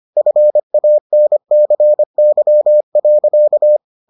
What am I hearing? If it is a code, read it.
Morse code, heard as a keyed tone, about 25 wpm, 600 Hz. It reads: FANCY.